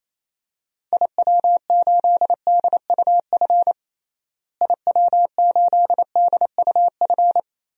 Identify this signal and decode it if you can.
Morse code, heard as a keyed tone, about 28 wpm, 700 Hz. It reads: IW8DUF IW8DUF
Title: DXCC-100 Call Signs - 6 Characters - Repeated 2x - 28wpm